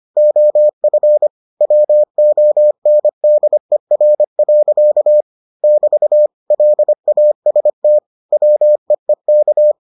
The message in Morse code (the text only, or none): OF WONDER. <BT> LAST WEEK